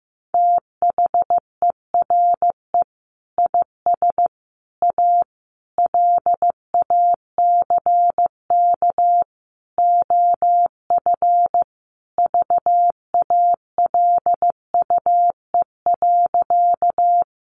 Morse code, heard as a keyed tone, about 15 wpm, 700 Hz. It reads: THERE IS A LACK OF VALUE.
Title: Sentences from Top 1100 Words 15wpm